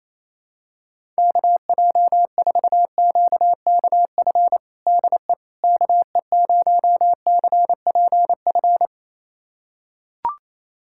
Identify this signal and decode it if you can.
Morse code, heard as a keyed tone, about 28 wpm, 700 Hz. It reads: KJ4QKF DE KE0CPF
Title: USA Callsign DE Callsign - 6 Characters - 28wpm